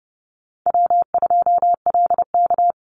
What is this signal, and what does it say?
Morse code, heard as a keyed tone, about 30 wpm, 700 Hz. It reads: W2LK
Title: Easy Call Signs - 30wpm